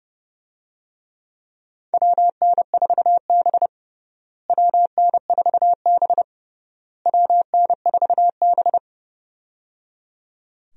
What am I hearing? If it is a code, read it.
Morse code, heard as a keyed tone, about 30 wpm, 700 Hz. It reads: WN4B WN4B WN4B
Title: USA Call Signs - 4 Characters - Repeated 3x - 30wpm